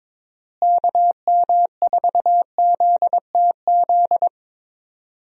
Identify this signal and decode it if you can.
Morse code, heard as a keyed tone, about 22 wpm, 700 Hz. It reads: KM4ZTZ
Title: USA Call Signs - 6 Characters - Repeated 3x - 22wpm